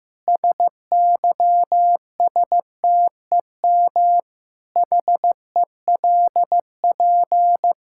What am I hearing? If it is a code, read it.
Morse code, heard as a keyed tone, about 15 wpm, 700 Hz. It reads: SYSTEM HELP
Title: Sentences from Top 1000 Words - Review Part 5 - 15wpm